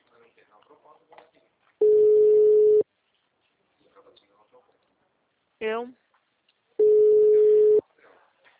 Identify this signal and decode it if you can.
Ringback tone (Europe, 425 Hz), with speech.